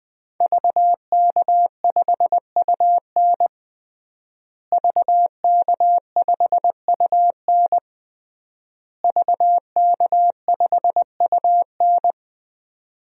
Morse code, heard as a keyed tone, about 21 wpm, 700 Hz. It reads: VK5UN VK5UN VK5UN
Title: DXCC-100 Call Signs - 5 Characters - Repeated 3x - 20wpm